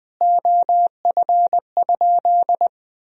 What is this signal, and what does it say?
Morse code, heard as a keyed tone, about 20 wpm, 700 Hz. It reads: OF?
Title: Sentences from Top 200 Words and Abbreviations 20wpm